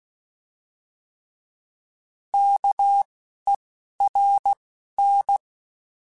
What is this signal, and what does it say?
Morse code, heard as a keyed tone, about 16 wpm, 800 Hz. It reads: K E R N
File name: Alphabetical / Alphabetical mp3s / Ks @10